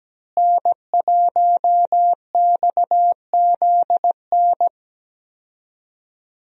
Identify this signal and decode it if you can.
Morse code, heard as a keyed tone, about 17 wpm, 700 Hz. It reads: N1XZN
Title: USA Call Signs - 5 Characters - Repeated 2x - 17wpm